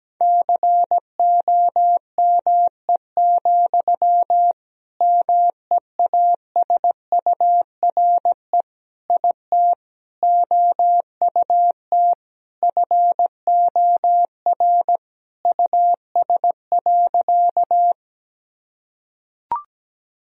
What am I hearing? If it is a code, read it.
Morse code, heard as a keyed tone, about 17 wpm, 700 Hz. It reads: COME, MEASURE IT OUT FOR US.
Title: Sentences from Top 1000 Words - Review Part 2 - 17wpm